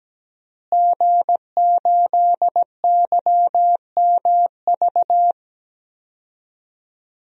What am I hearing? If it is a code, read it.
Morse code, heard as a keyed tone, about 17 wpm, 700 Hz. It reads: G8YMV
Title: DXCC-100 Call Signs - 5 Characters - Repeated 2x - 17wpm